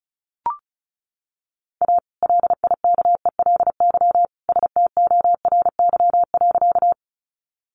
Morse code, heard as a keyed tone, about 34 wpm, 700 Hz. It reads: A LIKELY STORY.
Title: Sentences from Top 600 Words 35wpm